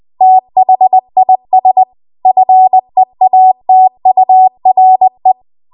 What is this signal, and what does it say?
Morse code, heard as a keyed tone, about 20 wpm, 750 Hz. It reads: THIS FEATURE
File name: Archive / 20 WPM / 20WPM